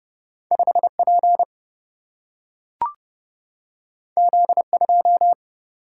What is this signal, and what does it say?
Morse code, heard as a keyed tone, about 30 wpm, 700 Hz. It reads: 5P Z2
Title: Single-Letter Single-Number 30wpm